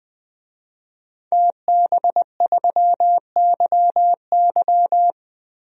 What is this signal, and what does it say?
Morse code, heard as a keyed tone, about 21 wpm, 700 Hz. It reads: TB3YY
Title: DXCC-100 Call Signs - 5 Characters - Repeated 2x - 20wpm